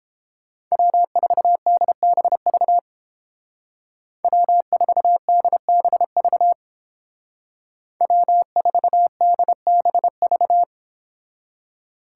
Morse code, heard as a keyed tone, about 30 wpm, 700 Hz. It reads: W4DBV W4DBV W4DBV